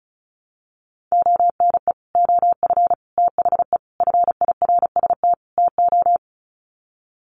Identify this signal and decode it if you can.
Morse code, heard as a keyed tone, about 34 wpm, 700 Hz. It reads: ONE OF THE FIRST TO
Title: Sets of 5 Words Encore 35wpm